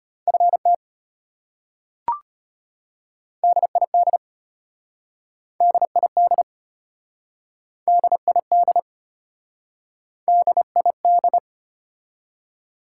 Morse code, heard as a keyed tone, about 33 wpm, 700 Hz. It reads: FT DID DID DID DID